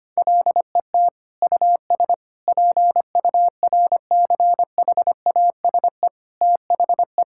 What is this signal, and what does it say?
Morse code, heard as a keyed tone, about 25 wpm, 700 Hz. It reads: LET US PURCHASE THE
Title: Sentences from Top 1000 Words - Review Part 6 - 25wpm